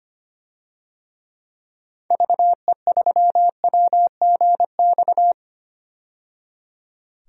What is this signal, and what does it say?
Morse code, heard as a keyed tone, about 25 wpm, 700 Hz. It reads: VE3WGX
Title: DXCC-100 Call Signs - 6 Characters - 25wpm